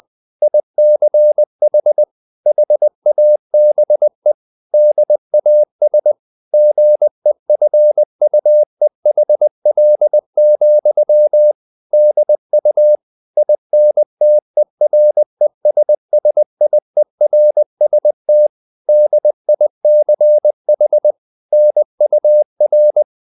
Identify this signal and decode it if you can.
Morse code, heard as a keyed tone, about 20 wpm, 600 Hz. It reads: ICH HABE DAS GEFUEHL, DU INTERESSIERST DICH NUR
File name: dk5ke / Humor-100BpM